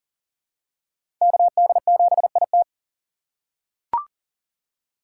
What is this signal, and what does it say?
Morse code, heard as a keyed tone, about 40 wpm, 700 Hz. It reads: KD7IT
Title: USA Call Signs - 5 Characters - 4x Character Spacing - 40wpm